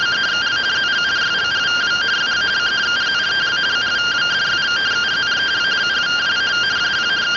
Two-tone teleprinter signal, about 170 Hz shift.